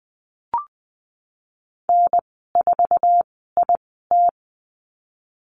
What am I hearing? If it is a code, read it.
Morse code, heard as a keyed tone, about 20 wpm, 700 Hz. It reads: N 4 I T
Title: 154 Lesson 25g - Letter p - Call Signs 20wpm